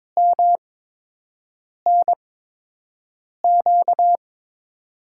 Morse code, heard as a keyed tone, about 22 wpm, spaced as for 6 wpm, 700 Hz. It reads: MNQ
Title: USA Call Signs - 8x Character Spacing - 22wpm